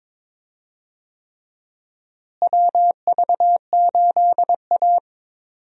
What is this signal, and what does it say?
Morse code, heard as a keyed tone, about 22 wpm, 700 Hz. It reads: WV8A
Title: USA Call Signs - 4 Characters - Repeated 3x - 22wpm